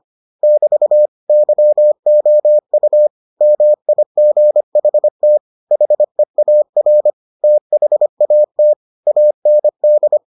Morse code, heard as a keyed tone, about 25 wpm, 600 Hz. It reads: <BT> YOU MIGHT HEAR THAT AND